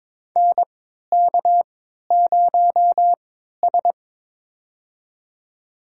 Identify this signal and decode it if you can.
Morse code, heard as a keyed tone, about 22 wpm, spaced as for 13 wpm, 700 Hz. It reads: NK0S
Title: USA Call Signs - 4 Characters - 3x Character Spacing - 22wpm